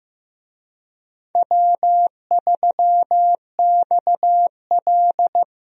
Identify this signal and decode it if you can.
Morse code, heard as a keyed tone, about 15 wpm, 700 Hz. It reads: W3XL
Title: USA Call Signs - 4 Characters - Repeated 3x - 15wpm